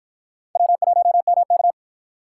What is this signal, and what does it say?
Morse code, heard as a keyed tone, about 65 wpm, 700 Hz. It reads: W1MK